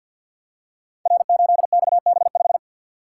Morse code, heard as a keyed tone, about 53 wpm, 700 Hz. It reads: R8XB5
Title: Code Groups - 5 Letter:Numbers - 50wpm